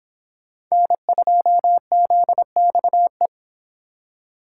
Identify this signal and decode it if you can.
Morse code, heard as a keyed tone, about 31 wpm, 700 Hz. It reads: N2ZXE